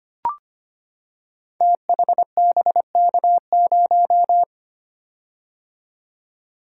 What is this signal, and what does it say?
Morse code, heard as a keyed tone, about 25 wpm, 700 Hz. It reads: THBK0
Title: Code Groups - 5 Letter:Numbers - 25wpm